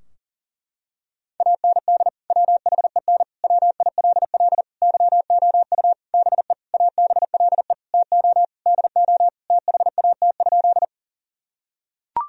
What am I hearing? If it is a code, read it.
Morse code, heard as a keyed tone, about 37 wpm, 700 Hz. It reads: AND WHEN WILL YOU BE ABLE TO DO THAT?
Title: Sentences from Top 1000 Words - Review Part 2 - 40wpm